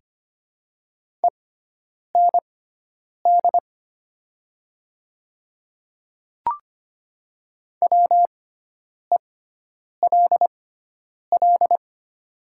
Morse code, heard as a keyed tone, about 25 wpm, spaced as for 9 wpm, 700 Hz. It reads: END WELL
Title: 6x Character Spacing - Top 200 Words - 25wpm